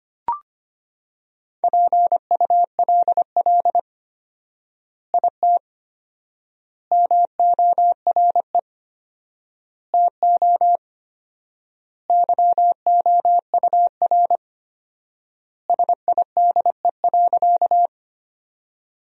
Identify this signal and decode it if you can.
Morse code, heard as a keyed tone, about 25 wpm, 700 Hz. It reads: PULL IT MORE TO YOUR SIDE.
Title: 4x Word-Spacing - Sentences from Top 400 Words - 25wpm